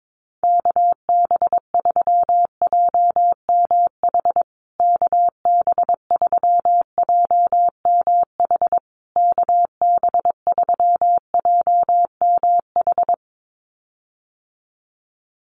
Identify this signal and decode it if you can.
Morse code, heard as a keyed tone, about 22 wpm, 700 Hz. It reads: KB3JMH KB3JMH KB3JMH
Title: USA Call Signs Repeated 3x 22wpm